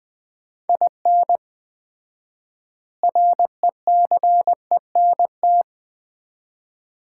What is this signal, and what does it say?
Morse code, heard as a keyed tone, about 21 wpm, 700 Hz. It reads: IN RECENT
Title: 4x Word-Spacing - Sets of 3 Words - 20wpm